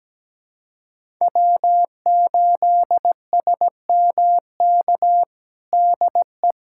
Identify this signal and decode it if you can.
Morse code, heard as a keyed tone, about 17 wpm, 700 Hz. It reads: W8SMK DE
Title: USA Callsign DE Callsign - 5 Characters - 17wpm